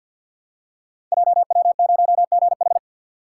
Morse code, heard as a keyed tone, about 50 wpm, 700 Hz. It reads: JW0GH
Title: Code Groups - 5 Letter:Numbers - 50wpm